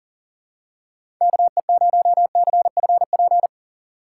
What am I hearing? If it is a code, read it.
Morse code, heard as a keyed tone, about 40 wpm, 700 Hz. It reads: KE0CFP
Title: USA Call Signs - 6 Characters - 4x Character Spacing - 40wpm